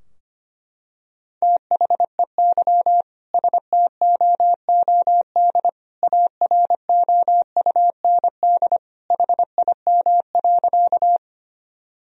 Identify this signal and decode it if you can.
Morse code, heard as a keyed tone, about 25 wpm, 700 Hz. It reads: THEY STOOD AROUND HIM.
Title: Sentences from Top 1000 Words - Review Part 3 - 25wpm